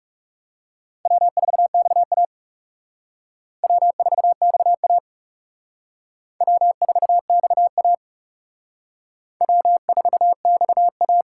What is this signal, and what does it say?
Morse code, heard as a keyed tone, about 35 wpm, 700 Hz. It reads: W4XA W4XA W4XA W4XA
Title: Speed Racing - USA 1x2 Call Signs 30wpm